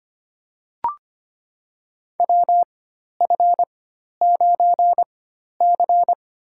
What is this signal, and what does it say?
Morse code, heard as a keyed tone, about 25 wpm, spaced as for 12 wpm, 700 Hz. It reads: WF9C
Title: USA Call Signs - 4x Character Spacing - 25wpm